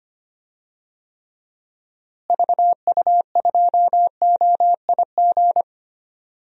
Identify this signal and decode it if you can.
Morse code, heard as a keyed tone, about 25 wpm, 700 Hz. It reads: VU2OIG
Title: DXCC-100 Call Signs - 6 Characters - Repeated 2x - 25wpm